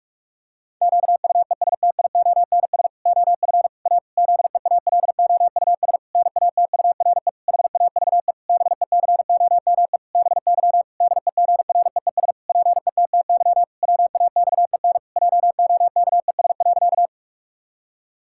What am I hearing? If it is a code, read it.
Morse code, heard as a keyed tone, about 46 wpm, 700 Hz. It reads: QUESTIONS OF A ZEALOUS NATURE HAVE BECOME BY DEGREES PETTY WAXEN JOKES.
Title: Pangram Challenge v2 - 45wpm